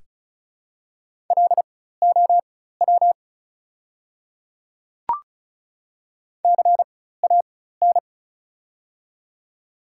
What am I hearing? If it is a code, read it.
Morse code, heard as a keyed tone, about 36 wpm, spaced as for 16 wpm, 700 Hz. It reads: LOW CAN
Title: 4x Character Spacing - Top 300 Words - 35wpm